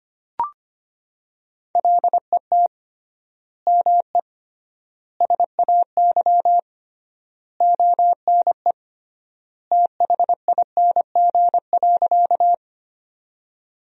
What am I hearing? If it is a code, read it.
Morse code, heard as a keyed tone, about 25 wpm, 700 Hz. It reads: LET ME SAY ONE THING.
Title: 3x Word-Spacing - Sentences from Top 300 Words - 25wpm